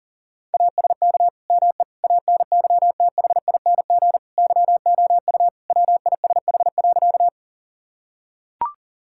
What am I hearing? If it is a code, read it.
Morse code, heard as a keyed tone, about 40 wpm, 700 Hz. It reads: ASK ME ANYTHING YOU WISH.
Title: Sentences from Top 1600 Words 40wpm